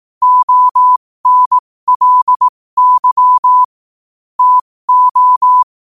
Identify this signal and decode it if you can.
Morse code, heard as a keyed tone, about 18 wpm, spaced as for 15 wpm, 1000 Hz. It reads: ONLY TO
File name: qotdfiles / 15wpm